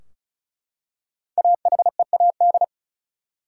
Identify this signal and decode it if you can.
Morse code, heard as a keyed tone, about 36 wpm, 700 Hz. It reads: AHEAD
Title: Five Letter Words - 35wpm